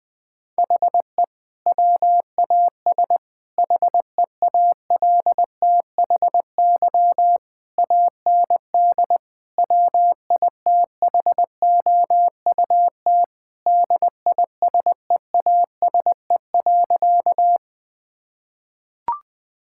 Morse code, heard as a keyed tone, about 20 wpm, 700 Hz. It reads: HE WAS HEALTHY AND WITHOUT DISEASE.
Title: Sentences from Top 1000 Words - Review Part 6 - 20wpm